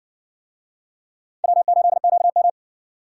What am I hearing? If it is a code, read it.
Morse code, heard as a keyed tone, about 54 wpm, 700 Hz. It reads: W8QM